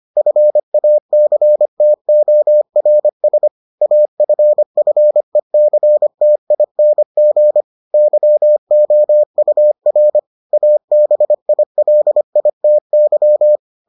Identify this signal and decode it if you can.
Morse code, heard as a keyed tone, about 25 wpm, 600 Hz. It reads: FACTORS AFFECTING YOUR ABILITY